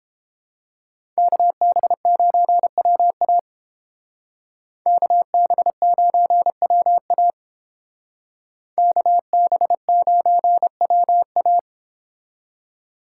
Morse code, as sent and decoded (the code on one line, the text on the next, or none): -.- -... ----. .-- .- / -.- -... ----. .-- .- / -.- -... ----. .-- .-
KB9WA KB9WA KB9WA